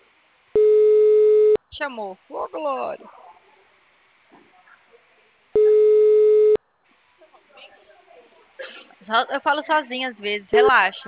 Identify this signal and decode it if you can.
Ringback tone (Europe, 425 Hz), with speech.
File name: out- -2012- - - .196